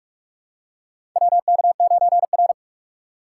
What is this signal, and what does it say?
Morse code, heard as a keyed tone, about 46 wpm, 700 Hz. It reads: WK9R
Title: USA Call Signs - 3x Character Spacing - 45wpm